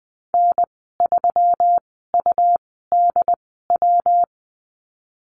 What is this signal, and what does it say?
Morse code, heard as a keyed tone, about 20 wpm, spaced as for 15 wpm, 700 Hz. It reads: N3UDW
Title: 122 Lesson 20g - Number 3 - Call Signs 20wpm